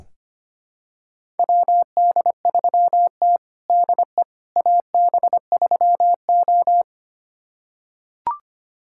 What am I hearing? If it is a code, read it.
Morse code, heard as a keyed tone, about 25 wpm, 700 Hz. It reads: WD3T DE AB3O
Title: USA Callsign DE Callsign - 4 Characters - 25wpm